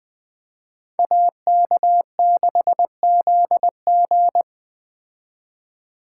Morse code, heard as a keyed tone, about 21 wpm, 700 Hz. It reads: AK6ZG
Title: Speed Racing - USA 2x2 Call Signs 17wpm